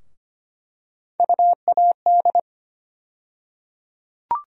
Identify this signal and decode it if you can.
Morse code, heard as a keyed tone, about 25 wpm, 700 Hz. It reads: UAD
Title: USA Call Sign Postfixes - 3 Letters 25wpm